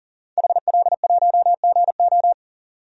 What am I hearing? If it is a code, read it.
Morse code, heard as a keyed tone, about 40 wpm, 700 Hz. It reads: HL1GO
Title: DXCC-100 Call Signs - 5 Characters - 40wpm